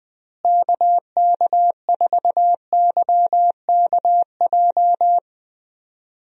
Morse code, heard as a keyed tone, about 21 wpm, 700 Hz. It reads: KK4YKJ
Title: USA Call Signs - 6 Characters - Repeated 2x - 20wpm